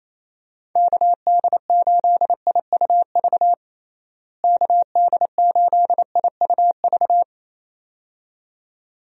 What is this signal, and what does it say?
Morse code, heard as a keyed tone, about 27 wpm, 700 Hz. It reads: KD8IUV KD8IUV
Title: USA Call Signs - 6 Characters - Repeated 2x - 28wpm